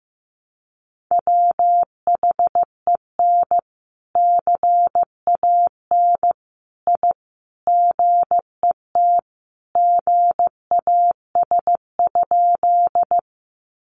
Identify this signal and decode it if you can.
Morse code, heard as a keyed tone, about 15 wpm, 700 Hz. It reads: WHEN CAN I GET GAS?
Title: Sentences from Top 1100 Words 15wpm